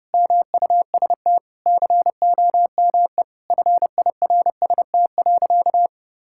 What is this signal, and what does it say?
Morse code, heard as a keyed tone, about 30 wpm, 700 Hz. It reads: MUST COME FIRST.
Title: Sentences from Top 1000 Words - Review Part 6 - 30wpm